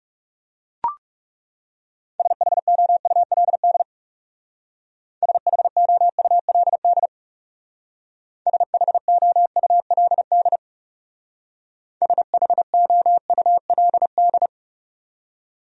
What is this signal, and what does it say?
Morse code, heard as a keyed tone, about 36 wpm, 700 Hz. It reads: SHOULD SHOULD SHOULD SHOULD
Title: Speed-Racing Top 300 Words 30wpm